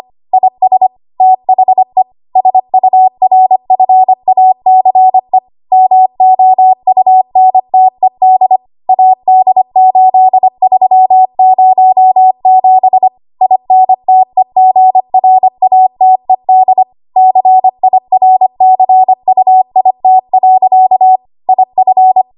Morse code, heard as a keyed tone, about 25 wpm, 750 Hz. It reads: IS THE SURFACE MOUNTED AD8307 INTEGRATED CIRCUIT. IF
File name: Archive / 25 WPM / 25WPM